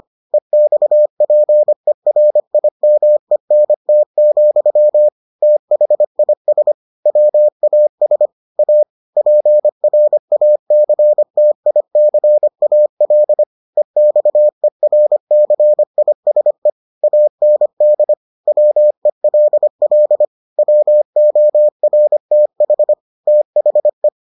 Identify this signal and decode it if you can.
Morse code, heard as a keyed tone, about 25 wpm, 600 Hz. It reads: EXPERIMENT, THIS WAS A PRACTICAL EXERCISE AND WELL WORTH THE